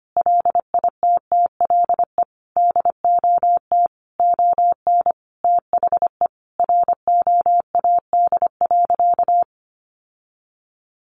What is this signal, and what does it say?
Morse code, heard as a keyed tone, about 25 wpm, 700 Hz. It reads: LITTLE DOT ON THE ROAD.
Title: Sentences from Top 1500 Words 25wpm